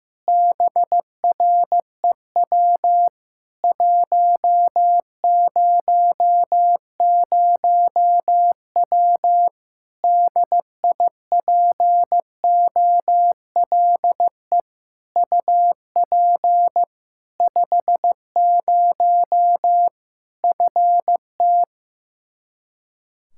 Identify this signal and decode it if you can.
Morse code, heard as a keyed tone, about 15 wpm, 700 Hz. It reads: BREW 100W DIPOLE UP 50 FT
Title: Practice QSO - Exchange 2 - Info - 15wpm